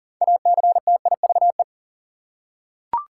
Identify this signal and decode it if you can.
Morse code, heard as a keyed tone, about 40 wpm, 700 Hz. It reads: ACTIVE
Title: Six Letter Words - 40wpm